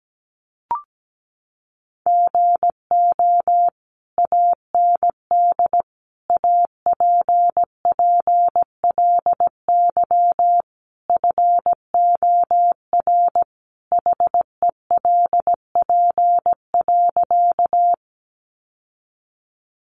Morse code, heard as a keyed tone, about 17 wpm, 700 Hz. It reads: GO AND APPLY FOR HELP.
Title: Sentences from Top 1200 Words 17wpm